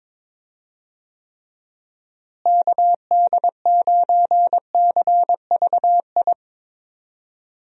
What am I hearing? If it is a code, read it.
Morse code, heard as a keyed tone, about 22 wpm, 700 Hz. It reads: KD9CVI
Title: USA Call Signs - 6 Characters - Repeated 3x - 22wpm